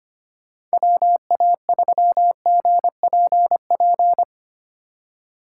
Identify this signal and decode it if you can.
Morse code, heard as a keyed tone, about 25 wpm, 700 Hz. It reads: WA3GPP
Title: K1USN SST QSO Practice - 25wpm